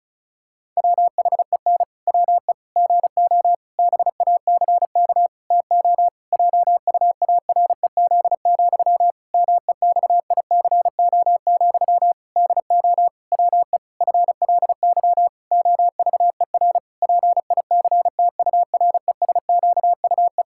Morse code, heard as a keyed tone, about 34 wpm, 700 Hz. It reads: WHEN WE GO BACK TO JUAREZ, MEXICO, DO WE FLY OVER PICTURESQUE
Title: Pangram Challenge v2 - 35wpm